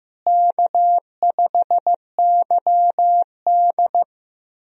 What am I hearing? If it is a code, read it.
Morse code, heard as a keyed tone, about 15 wpm, 700 Hz. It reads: K5YD
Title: USA Call Signs - 4 Characters - Repeated 3x - 15wpm